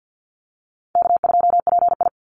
Morse code, heard as a keyed tone, about 50 wpm, 700 Hz. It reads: K2PI
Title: Easy Call Signs - 50wpm